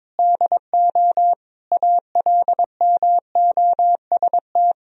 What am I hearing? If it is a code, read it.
Morse code, heard as a keyed tone, about 22 wpm, 700 Hz. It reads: DO ALMOST
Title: Sentences from Top 1000 Words - Review Part 4 - 22wpm